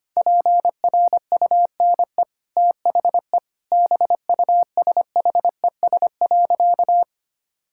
Morse code, heard as a keyed tone, about 25 wpm, 700 Hz. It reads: PRUNE THE BUSHES.
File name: Easy Sentences with Hints - 25wpm